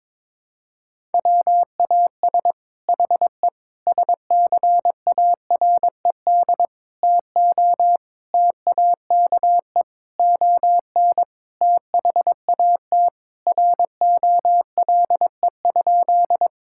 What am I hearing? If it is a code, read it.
Morse code, heard as a keyed tone, about 22 wpm, 700 Hz. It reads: WAS HE SCARED TO TAKE ON THAT ROLE?
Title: Sentences from Top 1000 Words - Review Part 6 - 22wpm